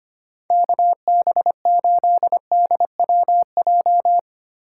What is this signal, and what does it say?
Morse code, heard as a keyed tone, about 31 wpm, 700 Hz. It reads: KB8DWJ
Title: Speed Racing - USA 2x3 Call Signs 25wpm